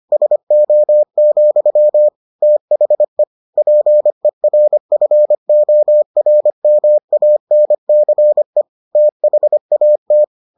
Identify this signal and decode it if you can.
Morse code, heard as a keyed tone, about 25 wpm, 600 Hz. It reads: SO, THE PERFORMANCE THAT